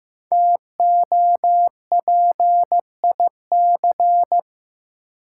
Morse code, heard as a keyed tone, about 15 wpm, 700 Hz. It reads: TOPIC
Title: Sentences from Top 1200 Words - New Word Drill Alt - 15wpm